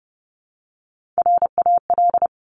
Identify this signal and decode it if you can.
Morse code, heard as a keyed tone, about 30 wpm, 700 Hz. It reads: RAL
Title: Top Three Letter Combinations in Words 30wpm